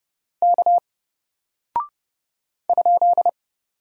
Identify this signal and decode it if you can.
Morse code, heard as a keyed tone, about 30 wpm, 700 Hz. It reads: K ?
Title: Single Character (Letter:Number:Punctuation:Prosign) Alt 30wpm